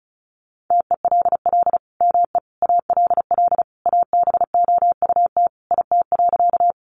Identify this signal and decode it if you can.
Morse code, heard as a keyed tone, about 34 wpm, 700 Hz. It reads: TELL ME ALL ABOUT IT.
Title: Sentences from Top 200 Words 35wpm